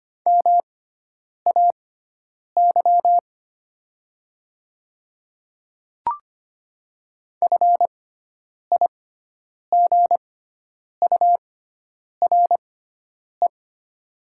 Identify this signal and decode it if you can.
Morse code, heard as a keyed tone, about 25 wpm, spaced as for 9 wpm, 700 Hz. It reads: MAY FIGURE
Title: 6x Character Spacing - Top 500 Words - 25wpm